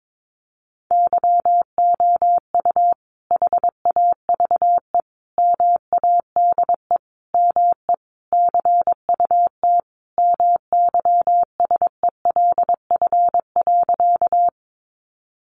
Morse code, heard as a keyed tone, about 22 wpm, 700 Hz. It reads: YOU HAVE MADE ME CUT MYSELF.
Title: Sentences from Top 1300 Words 22wpm